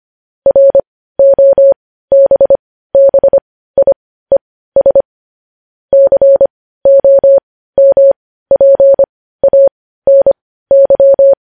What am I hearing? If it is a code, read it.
Morse code, heard as a keyed tone, about 25 wpm, spaced as for 15 wpm, 550 Hz. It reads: ROBBIES COMPANY